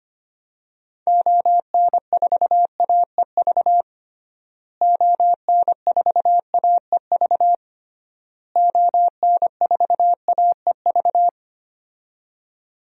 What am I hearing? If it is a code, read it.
Morse code, heard as a keyed tone, about 25 wpm, 700 Hz. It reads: ON4AEV ON4AEV ON4AEV
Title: DXCC-100 Call Signs - 6 Characters - Repeated 3x - 25wpm